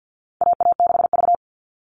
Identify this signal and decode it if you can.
Morse code, heard as a keyed tone, about 50 wpm, 700 Hz. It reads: AA6V
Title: All Cut Numbers - 4-Digit 50wpm